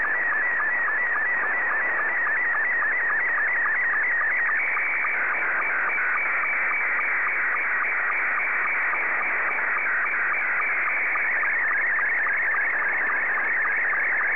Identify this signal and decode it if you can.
SSTV (slow-scan television), Scottie 2.
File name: sstv / S2-JA8FFD